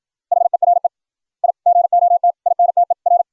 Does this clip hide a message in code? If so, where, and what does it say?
Yes: Morse code from 0.3 to 3.2 s, heard as a keyed tone, about 56 wpm, 700 Hz. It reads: HERE I GOT INTER